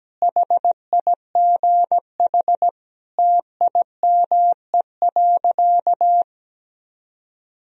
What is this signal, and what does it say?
Morse code, heard as a keyed tone, about 17 wpm, 700 Hz. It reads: HIGH TIME.
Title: Sentences from Top 1000 Words - Review Part 1 - 17wpm